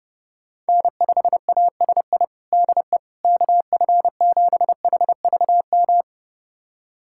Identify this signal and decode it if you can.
Morse code, heard as a keyed tone, about 30 wpm, 700 Hz. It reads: N5ASI DE KF7HVM
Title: USA Callsign DE Callsign 30wpm